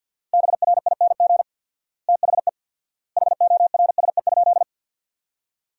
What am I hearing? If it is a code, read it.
Morse code, heard as a keyed tone, about 51 wpm, 700 Hz. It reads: BRING THE HORSE?
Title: 4x Word-Spacing - Sentences from Top 500 Words - 50wpm